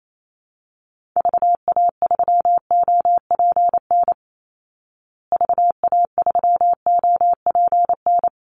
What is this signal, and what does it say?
Morse code, heard as a keyed tone, about 28 wpm, 700 Hz. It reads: VA3OPN VA3OPN
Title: Call Signs Repeated 3x - 28wpm